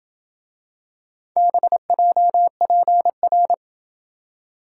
Morse code, heard as a keyed tone, about 27 wpm, 700 Hz. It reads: BJPR